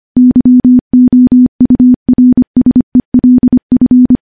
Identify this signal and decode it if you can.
Morse code, heard as a keyed tone, about 25 wpm, 250 Hz. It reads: YOURSELF
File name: morse / morse-training-25WPM